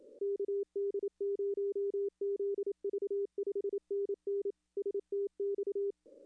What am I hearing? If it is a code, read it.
Morse code, heard as a keyed tone, about 27 wpm, 410 Hz. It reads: KD0ZV 5NN STX
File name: contests / arrl-160-k5tr-op-w5tn -2300